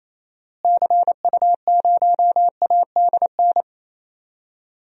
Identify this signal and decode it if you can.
Morse code, heard as a keyed tone, about 28 wpm, 700 Hz. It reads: CU0ADN
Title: DXCC-100 Call Signs - 6 Characters - Repeated 3x - 28wpm